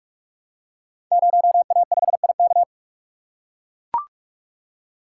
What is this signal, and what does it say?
Morse code, heard as a keyed tone, about 44 wpm, 700 Hz. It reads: 0A5IK
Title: Code Groups - 5 Letter:Numbers - 45wpm